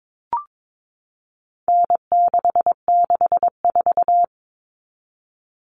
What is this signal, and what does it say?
Morse code, heard as a keyed tone, about 22 wpm, 700 Hz. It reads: N664
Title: All Cut Numbers - 4-Digit 22wpm